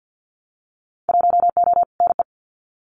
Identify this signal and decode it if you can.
Morse code, heard as a keyed tone, about 50 wpm, 700 Hz. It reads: 1O NE
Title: Winter Field Day Exchanges Encore 50wpm